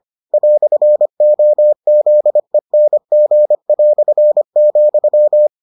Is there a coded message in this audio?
Yes: Morse code from 0.3 to 5.5 s, heard as a keyed tone, about 25 wpm, 600 Hz. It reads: "OZENG",